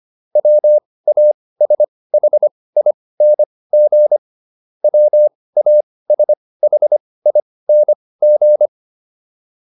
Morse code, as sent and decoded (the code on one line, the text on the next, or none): .-- .- ... .... .. -. --. / .-- .- ... .... .. -. --.
WASHING WASHING